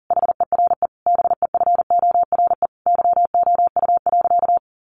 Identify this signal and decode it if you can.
Morse code, heard as a keyed tone, about 40 wpm, 700 Hz. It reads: HERE BEFORE YOU.
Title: Sentences from Top 1400 Words 40wpm